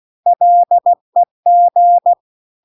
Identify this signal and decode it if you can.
Morse code, heard as a keyed tone, about 16 wpm, 700 Hz. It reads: LEG
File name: CW-The-Big-Twoheaded-River-2-16WPM